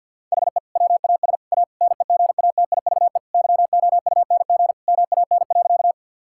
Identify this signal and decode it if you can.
Morse code, heard as a keyed tone, about 52 wpm, 700 Hz. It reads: HE WAS A NEGATIVE YOUNG MAN.
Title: Sentences from Top 1000 Words - Review Part 6 - 50wpm